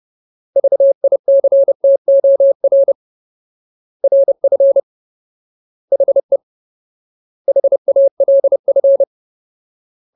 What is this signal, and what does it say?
Morse code, heard as a keyed tone, about 30 wpm, 550 Hz. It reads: VICTOR RF HE HALF
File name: Ultimate Word List / 30WPM Ultimate